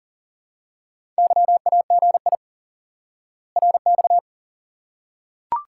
Morse code, heard as a keyed tone, about 40 wpm, 700 Hz. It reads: YAGI RX
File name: IQE Course 3 - Lesson 89 - YAGI - Two Elements - 40wpm